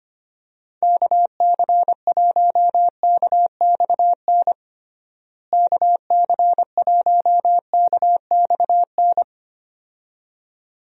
Morse code, heard as a keyed tone, about 25 wpm, 700 Hz. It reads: KC1KXN KC1KXN
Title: USA Call Signs - 6 Characters - Repeated 2x - 25wpm